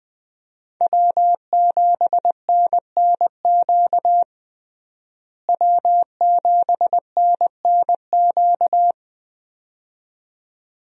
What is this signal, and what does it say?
Morse code, heard as a keyed tone, about 20 wpm, 700 Hz. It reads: W7NNQ W7NNQ
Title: USA Call Signs - 5 Characters - Repeated 2x - 20wpm